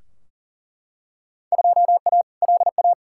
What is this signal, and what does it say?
Morse code, heard as a keyed tone, about 37 wpm, 700 Hz. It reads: 2A LA
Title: Field Day Exchanges Encore - 40wpm